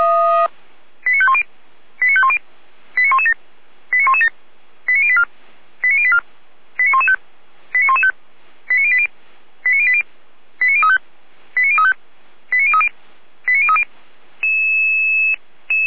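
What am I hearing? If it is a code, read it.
Five-tone selective call (ZVEI), heard as quick runs of five tones, tones of about 70 ms. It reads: 89510 89510 89117 89117 89084 89084 89115 89115 89080 89080 89035 89035 89030 89030
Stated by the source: ZVEI five-tone call 89092 received by SDR.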